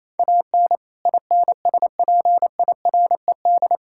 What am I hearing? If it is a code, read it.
Morse code, heard as a keyed tone, about 27 wpm, 700 Hz. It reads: AN INSPIRED
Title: Pangram Challenge v2 - 28wpm